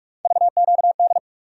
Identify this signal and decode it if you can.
Morse code, heard as a keyed tone, about 44 wpm, 700 Hz. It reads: VQD
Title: USA Call Sign Postfixes - 3 Letters 45wpm